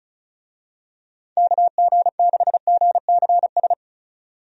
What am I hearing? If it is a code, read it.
Morse code, heard as a keyed tone, about 36 wpm, 700 Hz. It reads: KG6GCS
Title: USA Call Signs - 6 Characters - 35wpm